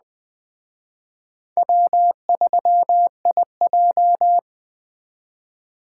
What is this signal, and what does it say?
Morse code, heard as a keyed tone, about 20 wpm, 700 Hz. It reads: W3IJ
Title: Speed Racing - USA Call Signs 17wpm